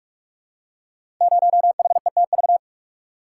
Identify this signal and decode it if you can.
Morse code, heard as a keyed tone, about 44 wpm, 700 Hz. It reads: 0HETV
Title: Code Groups - 5 Letter:Numbers - 45wpm